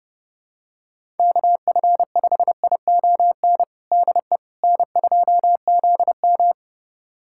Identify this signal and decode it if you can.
Morse code, heard as a keyed tone, about 30 wpm, 700 Hz. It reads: KF5ION DE N2ZM
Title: USA Callsign DE Callsign 30wpm